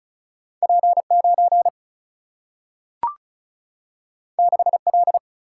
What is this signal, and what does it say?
Morse code, heard as a keyed tone, about 36 wpm, 700 Hz. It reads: P9 6L
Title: Single-Letter Single-Number 35wpm